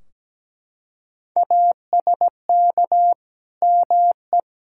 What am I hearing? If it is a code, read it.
Morse code, heard as a keyed tone, about 17 wpm, 700 Hz. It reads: ASK ME
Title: Sentences from Top 1600 Words 17wpm